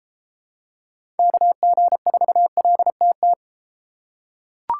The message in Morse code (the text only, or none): KG4LTT